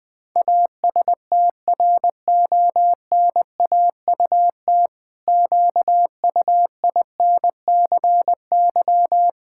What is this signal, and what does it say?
Morse code, heard as a keyed tone, about 21 wpm, 700 Hz. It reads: ASTRONAUT QUINCY
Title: Pangram Challenge v2 - 20wpm